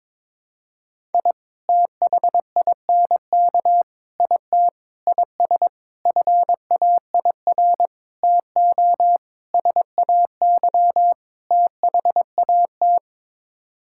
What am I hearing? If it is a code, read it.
Morse code, heard as a keyed tone, about 22 wpm, 700 Hz. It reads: I THINK IT IS FAIR TO SAY THAT
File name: practice / 22wpm / Sets of 8 Words - 22wpm